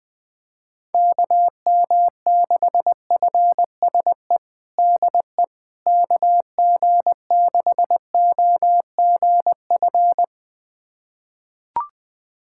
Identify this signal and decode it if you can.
Morse code, heard as a keyed tone, about 20 wpm, 700 Hz. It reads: KM6FSE DE KG6OGF
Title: USA Callsign DE Callsign - 6 Characters - 20wpm